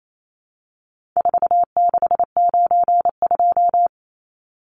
Morse code, heard as a keyed tone, about 28 wpm, 700 Hz. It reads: 4692
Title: Numbers - 4-Digit 28wpm